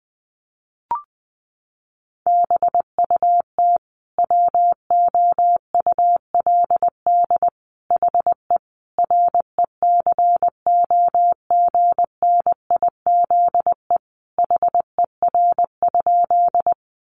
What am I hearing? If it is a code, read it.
Morse code, heard as a keyed tone, about 20 wpm, 700 Hz. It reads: BUT WOULD HE RECOGNIZE HER?
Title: Sentences from Top 1400 Words 20wpm